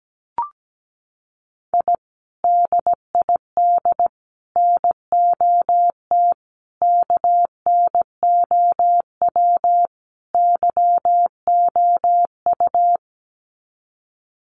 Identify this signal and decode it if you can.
Morse code, heard as a keyed tone, about 17 wpm, 700 Hz. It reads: I DID NOT KNOW YOU
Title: Sets of 5 Words Encore2 17wpm